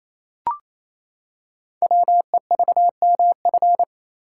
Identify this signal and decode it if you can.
Morse code, heard as a keyed tone, about 27 wpm, 700 Hz. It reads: WEVMF
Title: Code Groups - 5 Letters - 28wpm